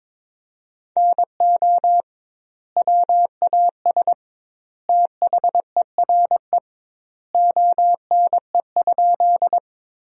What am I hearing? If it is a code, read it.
Morse code, heard as a keyed tone, about 21 wpm, 700 Hz. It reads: NO WAS THERE ONE?
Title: 2x Word-Spacing - Sentences from Top 100 Words - 22wpm